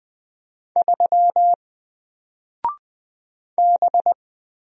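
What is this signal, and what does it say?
Morse code, heard as a keyed tone, about 20 wpm, 700 Hz. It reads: 3 B
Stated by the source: Single letter and number Morse drill at 20 wpm, from N4LCD.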